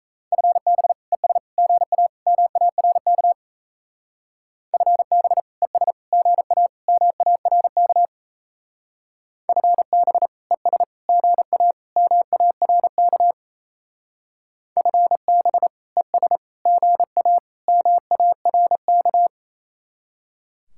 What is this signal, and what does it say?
Morse code, heard as a keyed tone, about 37 wpm, 700 Hz. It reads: FB ES GA MARK FB ES GA MARK FB ES GA MARK FB ES GA MARK